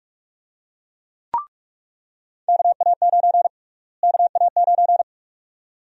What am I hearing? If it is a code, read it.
Morse code, heard as a keyed tone, about 46 wpm, 700 Hz. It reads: KA9 KA9
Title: USA Call Sign Prefixes - Repeated 2x - 45wpm